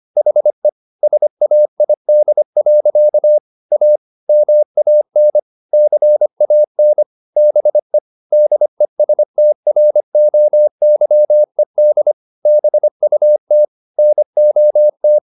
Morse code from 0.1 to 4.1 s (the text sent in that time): HE SAID. A